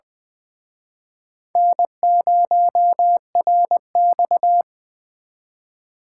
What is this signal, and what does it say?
Morse code, heard as a keyed tone, about 23 wpm, 700 Hz. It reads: N0RX
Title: Speed Racing - USA 1x2 Call Signs 20wpm